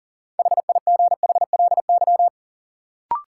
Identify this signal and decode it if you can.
Morse code, heard as a keyed tone, about 40 wpm, 700 Hz. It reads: HIGHLY
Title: Sentences from Top 1100 Words - New Word Drill Alt - 40wpm